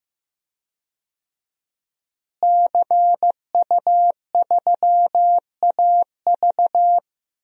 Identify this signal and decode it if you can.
Morse code, heard as a keyed tone, about 15 wpm, 700 Hz. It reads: CU3AV
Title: DXCC-100 Call Signs - 5 Characters - Repeated 2x - 15wpm